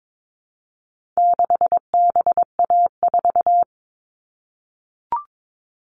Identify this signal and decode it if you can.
Morse code, heard as a keyed tone, about 22 wpm, 700 Hz. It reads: 6BA4
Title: All Cut Numbers - 4-Digit 22wpm